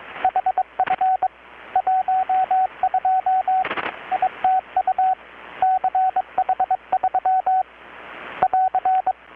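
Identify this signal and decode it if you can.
Morse code, heard as a keyed tone, about 22 wpm, 750 Hz, with radio noise. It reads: HF 12 ITU CH3 <AR>
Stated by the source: Morse from KPH's Night of Nights XX maritime radiotelegraphy event, received in Chiapas, Mexico about 3,800 km away on a Kenwood R-600 with a grounded 10 m long-wire antenna.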